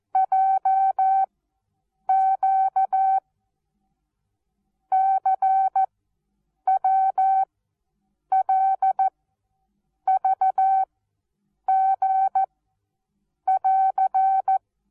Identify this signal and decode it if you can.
Morse code, heard as a keyed tone, about 14 wpm, 770 Hz. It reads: J Q C W L V G <AR>
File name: Onlinelehrgang / morsen / DJ4UF Morsen19